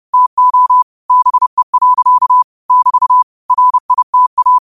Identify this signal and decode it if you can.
Morse code, heard as a keyed tone, about 30 wpm, 1000 Hz. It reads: TO BE. <BT> RITA